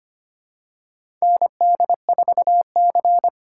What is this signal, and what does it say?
Morse code, heard as a keyed tone, about 25 wpm, 700 Hz. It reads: ND4C
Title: USA Call Signs - 4 Characters - 2x Character Spacing - 25wpm